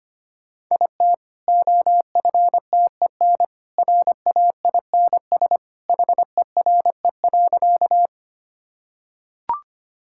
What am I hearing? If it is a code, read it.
Morse code, heard as a keyed tone, about 25 wpm, 700 Hz. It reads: IT OFTEN RAINS HERE.
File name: Easy Sentences with Hints - 25wpm